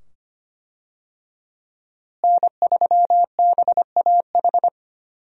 Morse code, heard as a keyed tone, about 25 wpm, 700 Hz. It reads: N3BAH
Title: USA Call Signs - 5 Characters - Repeated 2x - 25wpm